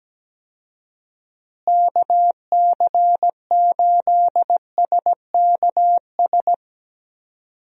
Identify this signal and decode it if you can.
Morse code, heard as a keyed tone, about 17 wpm, 700 Hz. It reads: KC8SKS
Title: USA Call Signs - 6 Characters - Repeated 3x - 17wpm